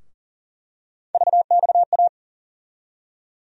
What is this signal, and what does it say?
Morse code, heard as a keyed tone, about 40 wpm, 700 Hz. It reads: VXA
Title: USA Call Sign Postfixes - 3 Letters 40wpm